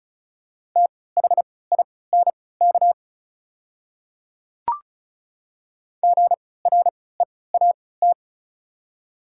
Morse code, heard as a keyed tone, about 36 wpm, spaced as for 20 wpm, 690 Hz. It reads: THINK GREAT
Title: 3x Character Spacing - Top 200 Words - 35wpm